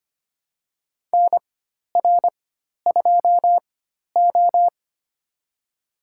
Morse code, heard as a keyed tone, about 25 wpm, spaced as for 12 wpm, 700 Hz. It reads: NR2O
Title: USA Call Signs - 4 Characters - 4x Character Spacing - 25wpm